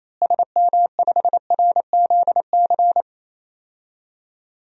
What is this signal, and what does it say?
Morse code, heard as a keyed tone, about 28 wpm, 700 Hz. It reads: SM5RZC
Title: DXCC-100 Call Signs - 6 Characters - Repeated 2x - 28wpm